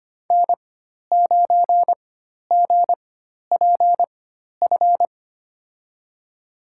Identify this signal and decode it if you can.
Morse code, heard as a keyed tone, about 25 wpm, spaced as for 12 wpm, 700 Hz. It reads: N9GPF
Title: USA Call Signs - 5 Characters - 4x Character Spacing - 25wpm